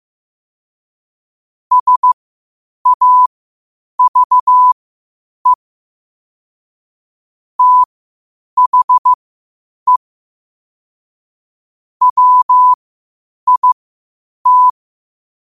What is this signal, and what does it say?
Morse code, heard as a keyed tone, about 15 wpm, spaced as for 8 wpm, 1000 Hz. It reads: SAVE THE WIT
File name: qotdfiles / 8wpm